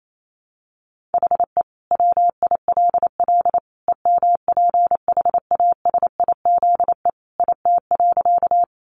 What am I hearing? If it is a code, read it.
Morse code, heard as a keyed tone, about 28 wpm, 700 Hz. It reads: HE WILL EMPHASIZE IT.
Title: Sentences from Top 1500 Words 28wpm